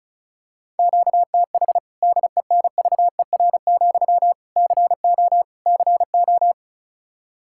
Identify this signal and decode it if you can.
Morse code, heard as a keyed tone, about 36 wpm, 700 Hz. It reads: QTH DENVER, CO CO
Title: Practice QSO - Exchange 1 - Beginning - 35wpm